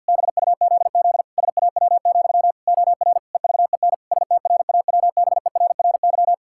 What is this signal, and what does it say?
Morse code, heard as a keyed tone, about 52 wpm, 700 Hz. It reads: BUZZ SAW, OR EVEN STRAWBERRY
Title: Pangram Challenge v2 - 50wpm